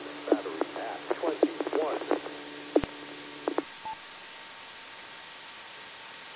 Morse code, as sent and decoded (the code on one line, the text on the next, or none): none